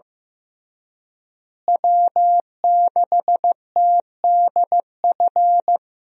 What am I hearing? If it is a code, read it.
Morse code, heard as a keyed tone, about 15 wpm, 700 Hz. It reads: W6TDF
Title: USA Call Signs - 5 Characters - Repeated 3x - 15wpm